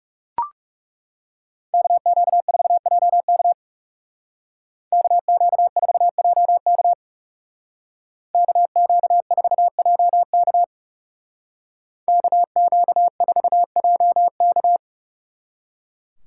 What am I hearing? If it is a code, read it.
Morse code, heard as a keyed tone, about 36 wpm, 700 Hz. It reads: KQ4JK KQ4JK KQ4JK KQ4JK
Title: Speed Racing - USA 2x2 Call Signs 30wpm